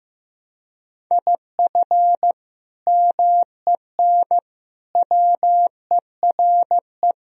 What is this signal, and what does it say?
Morse code, heard as a keyed tone, about 16 wpm, 700 Hz. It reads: IF MEN WERE
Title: Sentences from Top 1000 Words - Review Part 3 - 15wpm